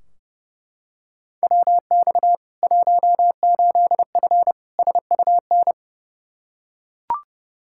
Morse code, heard as a keyed tone, about 30 wpm, 700 Hz. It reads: WX 18F SUN
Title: Practice QSO - Exchange 2 - Info - 30wpm